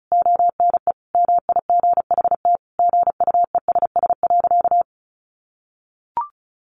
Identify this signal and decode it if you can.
Morse code, heard as a keyed tone, about 34 wpm, 700 Hz. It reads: ONE MIGHT GUESS.
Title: Sentences from Top 1500 Words 35wpm